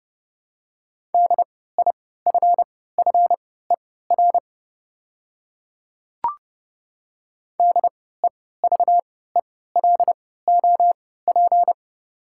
Morse code, heard as a keyed tone, about 30 wpm, spaced as for 17 wpm, 700 Hz. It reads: DIFFER DEVELOP
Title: 3x Character Spacing - Top 500 Words - 30wpm